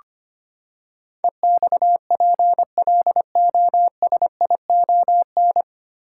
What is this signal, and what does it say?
Morse code, heard as a keyed tone, about 25 wpm, 700 Hz. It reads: EXPLOSION
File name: Nine Letter Words - 25wpm